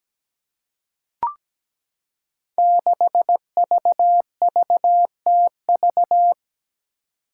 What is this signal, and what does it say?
Morse code, heard as a keyed tone, about 17 wpm, 700 Hz. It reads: 6VVTV
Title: Five Challenging Letter-Number Combinations 17wpm